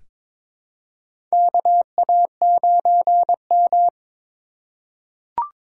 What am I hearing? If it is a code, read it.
Morse code, heard as a keyed tone, about 22 wpm, 700 Hz. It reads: KA9M
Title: USA Call Signs - 4 Characters - 4x Character Spacing - 22wpm